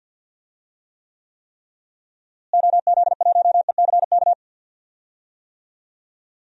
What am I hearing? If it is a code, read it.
Morse code, heard as a keyed tone, about 52 wpm, 700 Hz. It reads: OZ1ECK
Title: DXCC-100 Call Signs - 6 Characters - Repeated 3x - 50wpm